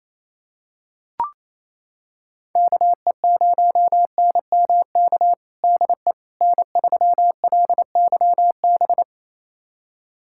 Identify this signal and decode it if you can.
Morse code, heard as a keyed tone, about 28 wpm, 700 Hz. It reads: KE0NMK DE N3LYB
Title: USA Callsign DE Callsign 28wpm